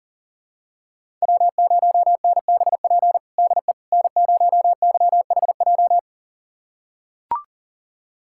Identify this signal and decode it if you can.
Morse code, heard as a keyed tone, about 40 wpm, 700 Hz. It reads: W0NBP DE N0YHJ
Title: USA Callsign DE Callsign - 5 Characters - 40wpm